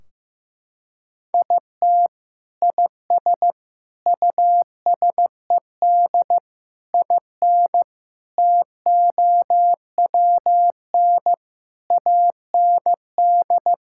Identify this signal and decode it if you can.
Morse code, heard as a keyed tone, about 15 wpm, 700 Hz. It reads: IT IS USED IN TOWN AND
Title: Sentences from Top 1600 Words 15wpm